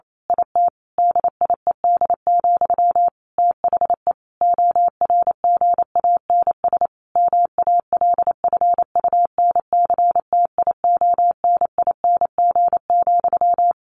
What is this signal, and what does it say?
Morse code, heard as a keyed tone, about 28 wpm, 700 Hz. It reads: IT DIED, THE ORGANS MALFUNCTIONING,
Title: A Sound Of Thunder 28wpm